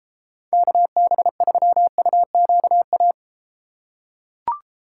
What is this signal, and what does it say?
Morse code, heard as a keyed tone, about 33 wpm, 700 Hz. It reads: KB3UQA